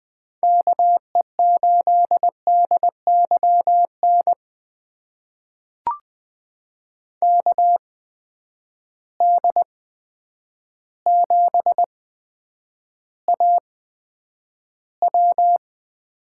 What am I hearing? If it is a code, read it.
Morse code, heard as a keyed tone, about 20 wpm, spaced as for 9 wpm, 700 Hz. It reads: KE8DYN KD7AW
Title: USA Call Signs - 6 Characters - 8x Character Spacing - 20wpm